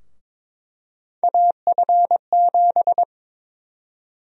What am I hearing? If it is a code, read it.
Morse code, heard as a keyed tone, about 22 wpm, 700 Hz. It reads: AF7
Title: USA Call Sign Prefixes - 22wpm